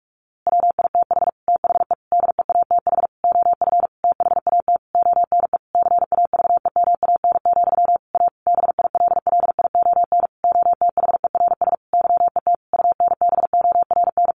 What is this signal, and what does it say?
Morse code, heard as a keyed tone, about 44 wpm, 700 Hz. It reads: WITH THE DEATH OF THAT ONE CAVEMAN, A BILLION OTHERS YET UNBORN
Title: A Sound Of Thunder 45wpm